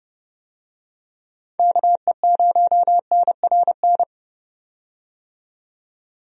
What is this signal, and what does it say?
Morse code, heard as a keyed tone, about 32 wpm, 700 Hz. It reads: KE0NRN
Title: USA Call Signs - 6 Characters - 30wpm